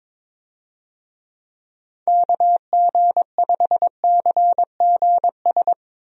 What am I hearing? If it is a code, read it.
Morse code, heard as a keyed tone, about 22 wpm, 700 Hz. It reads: KG5CGS
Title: USA Call Signs - 6 Characters - Repeated 3x - 22wpm